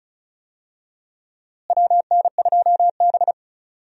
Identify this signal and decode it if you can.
Morse code, heard as a keyed tone, about 36 wpm, 700 Hz. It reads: WN2B
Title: USA Call Signs - 4 Characters - Repeated 3x - 35wpm